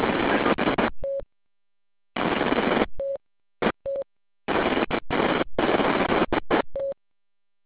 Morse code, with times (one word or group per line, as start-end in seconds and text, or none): none